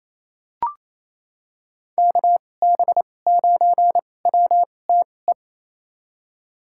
Morse code, heard as a keyed tone, about 27 wpm, spaced as for 20 wpm, 700 Hz. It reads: KB9WTE
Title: USA Call Signs - 6 Characters - 2x Character Spacing - 28wpm